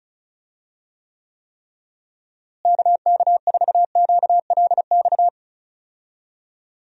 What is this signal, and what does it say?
Morse code, heard as a keyed tone, about 36 wpm, 700 Hz. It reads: KK4QLX
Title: USA Call Signs - 6 Characters - Repeated 3x - 35wpm